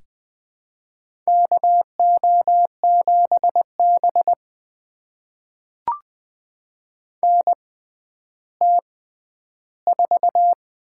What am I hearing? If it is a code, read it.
Morse code, heard as a keyed tone, about 20 wpm, spaced as for 11 wpm, 700 Hz. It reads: KO7B NT4
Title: USA Call Signs - 4 Characters - 6x Character Spacing - 20wpm